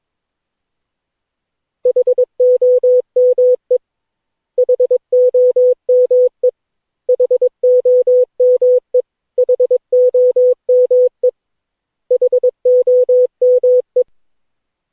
Morse code, heard as a keyed tone, about 22 wpm, 500 Hz. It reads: HOME HOME HOME HOME HOME
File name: cw / Session9 cw training